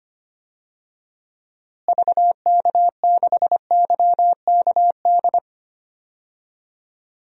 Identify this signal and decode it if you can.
Morse code, heard as a keyed tone, about 25 wpm, 700 Hz. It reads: VK6YKD
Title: DXCC-100 Call Signs - 6 Characters - Repeated 2x - 25wpm